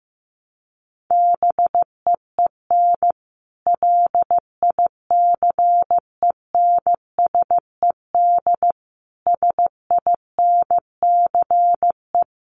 Morse code, heard as a keyed tone, about 15 wpm, 700 Hz. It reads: BEEN LICENSED SINCE
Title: Common QSO Phrases 15wpm